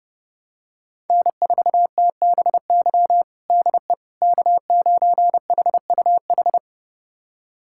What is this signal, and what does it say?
Morse code, heard as a keyed tone, about 30 wpm, 700 Hz. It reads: N4TBY DE K9HUH
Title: USA Callsign DE Callsign - 5 Characters - 30wpm